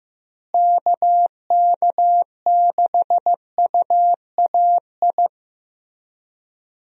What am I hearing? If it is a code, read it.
Morse code, heard as a keyed tone, about 15 wpm, 700 Hz. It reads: KK6UAI
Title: USA Call Signs - 6 Characters - Repeated 3x - 15wpm